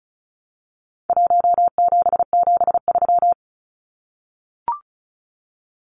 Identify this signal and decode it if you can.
Morse code, heard as a keyed tone, about 34 wpm, 700 Hz. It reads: 1773
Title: Numbers - Contest Style - 4-Digit 35wpm